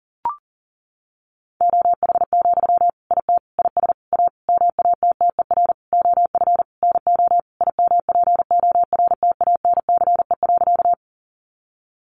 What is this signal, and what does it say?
Morse code, heard as a keyed tone, about 40 wpm, 700 Hz. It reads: OH, IT IS A MATTER OF NO IMPORTANCE.
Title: Sentences from Top 1200 Words 40wpm